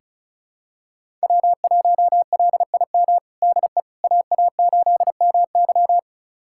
Morse code, heard as a keyed tone, about 36 wpm, 700 Hz. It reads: W1LIM DE AA8MY
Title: USA Callsign DE Callsign - 5 Characters - 35wpm